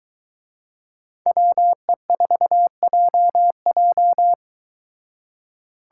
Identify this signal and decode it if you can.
Morse code, heard as a keyed tone, about 22 wpm, 700 Hz. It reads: WE4JJ